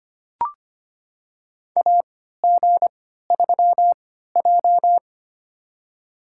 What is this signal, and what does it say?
Morse code, heard as a keyed tone, about 25 wpm, spaced as for 14 wpm, 700 Hz. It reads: AG3J
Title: USA Call Signs - 4 Characters - 3x Character Spacing - 25wpm